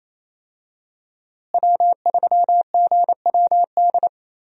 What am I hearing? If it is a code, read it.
Morse code, heard as a keyed tone, about 28 wpm, 700 Hz. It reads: W3GWD
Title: USA Call Signs - 5 Characters - Repeated 2x - 28wpm